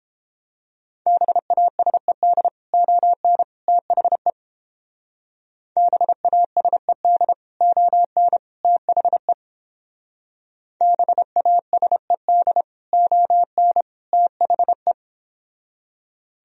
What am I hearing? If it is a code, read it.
Morse code, heard as a keyed tone, about 30 wpm, 700 Hz. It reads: BASED ON THE BASED ON THE BASED ON THE